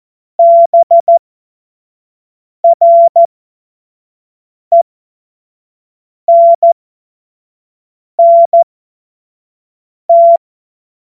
Morse code, heard as a keyed tone, about 14 wpm, spaced as for 5 wpm, 680 Hz. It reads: BRENNT
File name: Witz-26-August-2025-25-F70